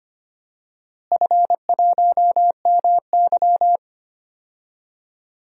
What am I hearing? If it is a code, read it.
Morse code, heard as a keyed tone, about 25 wpm, 700 Hz. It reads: F1MY
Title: DXCC-100 Call Signs - 4 Characters - Repeated 2x - 25wpm